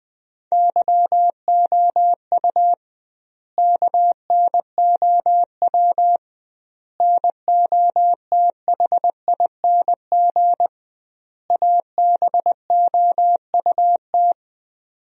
Morse code, heard as a keyed tone, about 20 wpm, 700 Hz. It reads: YOU KNOW NOTHING ABOUT
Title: 2x Word-Spacing - Sentences from Top 500 Words - 20wpm